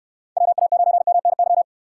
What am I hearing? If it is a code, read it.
Morse code, heard as a keyed tone, about 67 wpm, 700 Hz. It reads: WA9MNY